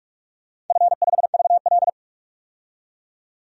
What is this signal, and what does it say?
Morse code, heard as a keyed tone, about 46 wpm, 700 Hz. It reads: F5VL
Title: DXCC-100 Call Signs - 4 Characters - 45wpm